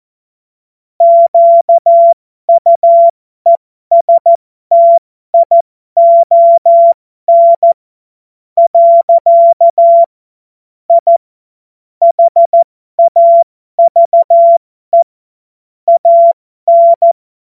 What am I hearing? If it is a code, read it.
Morse code, heard as a keyed tone, about 14 wpm, spaced as for 11 wpm, 680 Hz. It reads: QUESTION. I HAVE AN
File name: Morsekurs / Witze